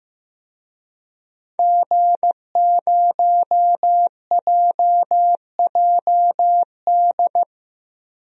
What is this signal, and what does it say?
Morse code, heard as a keyed tone, about 15 wpm, 700 Hz. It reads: G0JJD
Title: DXCC-100 Call Signs - 5 Characters - 15wpm